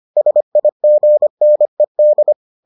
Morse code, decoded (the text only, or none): SIGNED